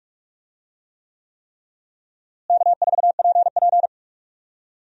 Morse code, heard as a keyed tone, about 46 wpm, 700 Hz. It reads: K4PP
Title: USA Call Signs - 4 Characters - Repeated 3x - 45wpm